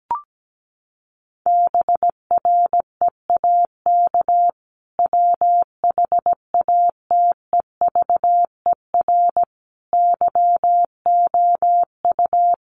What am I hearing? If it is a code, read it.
Morse code, heard as a keyed tone, about 17 wpm, 700 Hz. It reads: BREAK WHATEVER YOU
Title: Sentences from Top 1500 Words 17wpm